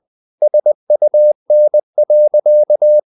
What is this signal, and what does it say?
Morse code, heard as a keyed tone, about 20 wpm, 600 Hz. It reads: SUN.
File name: the old man and the sea part 2 20wpm